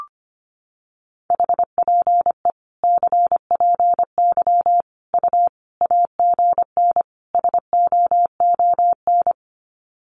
Morse code, heard as a keyed tone, about 25 wpm, 700 Hz. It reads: HPE CPY U AGN SOON
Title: Common QSO Phrases 25wpm